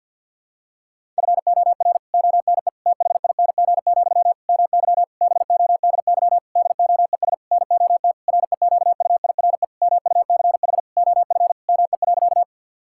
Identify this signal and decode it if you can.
Morse code, heard as a keyed tone, about 46 wpm, 700 Hz. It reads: FOR ONE THING, MY BODY DOES NOT REQUIRE MUCH OF ME.
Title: Sentences from Top 1000 Words - Review Part 6 - 50wpm